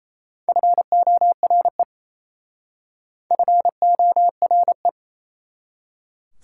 Morse code, heard as a keyed tone, about 36 wpm, 700 Hz. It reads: FORE FORE